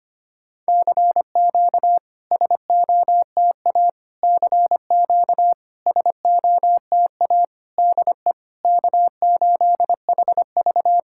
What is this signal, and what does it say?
Morse code, heard as a keyed tone, about 25 wpm, 690 Hz. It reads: CQ SOTA CQ SOTA DE K8HV
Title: SOTA QSO Practice 25wpm